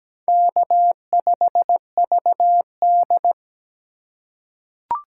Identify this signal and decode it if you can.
Morse code, heard as a keyed tone, about 17 wpm, 700 Hz. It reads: K5VD
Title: USA Call Signs - 2x Character Spacing - 17wpm